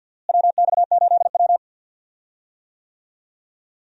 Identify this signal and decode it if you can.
Morse code, heard as a keyed tone, about 52 wpm, 700 Hz. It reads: WX8W
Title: USA Call Signs - 4 Characters - Repeated 3x - 50wpm